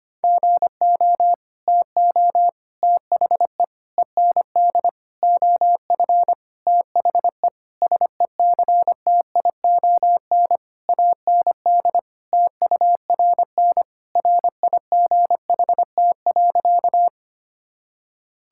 Morse code, heard as a keyed tone, about 25 wpm, 700 Hz. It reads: GO TO THE END OF THE SECTION AND TURN RIGHT.
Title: Sentences from Top 1000 Words - Review Part 5 - 25wpm